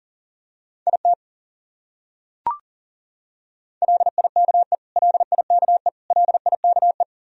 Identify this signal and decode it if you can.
Morse code, heard as a keyed tone, about 40 wpm, 700 Hz. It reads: IT LIKE LIKE LIKE
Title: Top 100 Words Repeated 40wpm